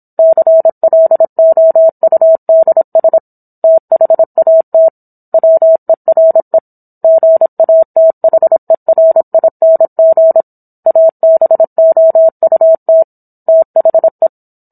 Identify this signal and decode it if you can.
Morse code, heard as a keyed tone, about 27 wpm, 650 Hz. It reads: CLOUDS THAT WERE GATHERING ABOUT THE
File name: Book One / 17-Chapter 17 - 26 WPM